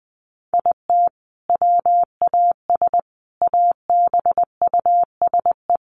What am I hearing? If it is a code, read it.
Morse code, heard as a keyed tone, about 20 wpm, 700 Hz. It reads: IT WAS ABUSE
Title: Sentences from Top 1300 Words 20wpm